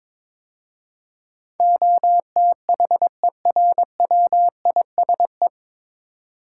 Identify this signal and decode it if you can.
Morse code, heard as a keyed tone, about 22 wpm, 700 Hz. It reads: OTHERWISE
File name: 9 Letter Words - First 3 Letters Given - 22wpm